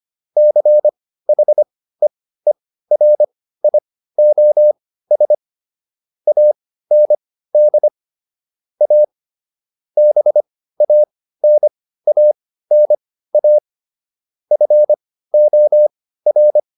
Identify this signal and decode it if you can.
Morse code, heard as a keyed tone, about 25 wpm, spaced as for 15 wpm, 600 Hz. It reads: CHEERIOS AND A BANANA FOR